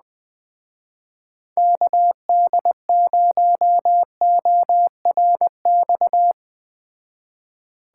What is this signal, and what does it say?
Morse code, heard as a keyed tone, about 21 wpm, 700 Hz. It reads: KD0ORX
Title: Speed Racing - USA Call Signs 17wpm